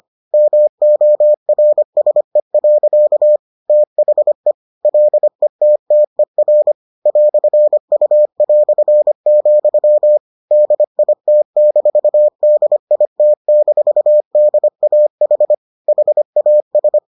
Morse code, heard as a keyed tone, about 25 wpm, 600 Hz. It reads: MORSE. THE LETTER "U", DIT-DIT-DAH HAS